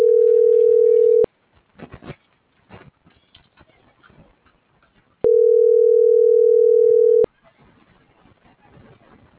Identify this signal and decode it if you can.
Ringback tone (North America), 440 + 480 Hz.